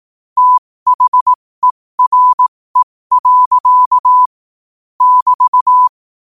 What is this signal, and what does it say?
Morse code, heard as a keyed tone, about 18 wpm, spaced as for 15 wpm, 1000 Hz. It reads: THERE. <BT>
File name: qotdfiles / 15wpm